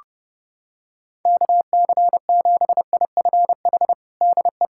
Morse code, heard as a keyed tone, about 30 wpm, 700 Hz. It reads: KC7IFH DE
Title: USA Callsign DE Callsign 30wpm